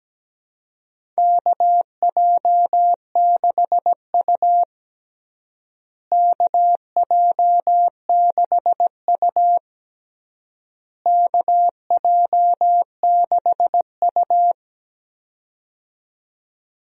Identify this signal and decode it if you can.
Morse code, heard as a keyed tone, about 17 wpm, 700 Hz. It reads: KJ6U KJ6U KJ6U
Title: USA Call Signs - 4 Characters - Repeated 3x - 17wpm